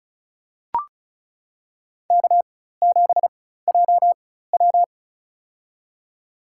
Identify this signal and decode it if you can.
Morse code, heard as a keyed tone, about 36 wpm, spaced as for 16 wpm, 700 Hz. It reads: K7JW
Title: USA Call Signs - 4 Characters - 4x Character Spacing - 35wpm